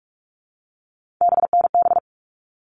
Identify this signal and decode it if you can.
Morse code, heard as a keyed tone, about 44 wpm, 700 Hz. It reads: BNB
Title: All Cut Numbers - 3-Digit 45wpm